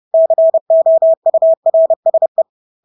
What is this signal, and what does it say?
Morse code, heard as a keyed tone, about 30 wpm, 650 Hz. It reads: COURSE
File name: Engelske ord 30-30 wpm